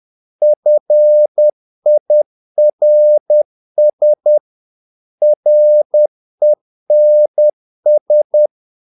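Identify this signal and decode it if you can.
Morse code, heard as a keyed tone, about 10 wpm, 600 Hz. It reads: FIRS RENS
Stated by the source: Morse practice of short Danish words at 10 wpm, from CW med Gnister.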